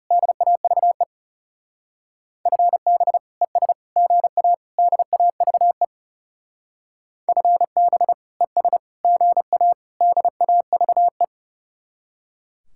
Morse code, heard as a keyed tone, about 37 wpm, 700 Hz. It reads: DAVE FB ES GA DAVE FB ES GA DAVE
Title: Speed-Racing Common QSO Phrases - 30wpm